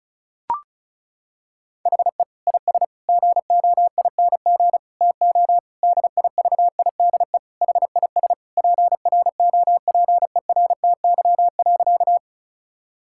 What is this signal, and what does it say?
Morse code, heard as a keyed tone, about 34 wpm, 700 Hz. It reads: HE IS GOING TO DIVIDE HIS PROPERTY.
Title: Sentences from Top 1600 Words 35wpm